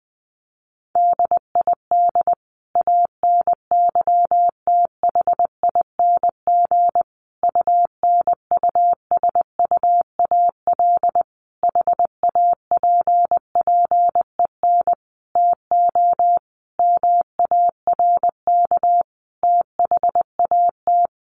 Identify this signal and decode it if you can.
Morse code, heard as a keyed tone, about 20 wpm, 700 Hz. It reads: DID ANYTHING UNUSUAL HAPPEN TO MARK THAT
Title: Sentences from Top 1400 Words 20wpm